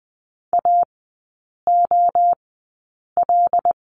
Morse code, heard as a keyed tone, about 20 wpm, spaced as for 8 wpm, 700 Hz. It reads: AOL
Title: 074 Lesson 13e - Letter L - Three Characters 20wpm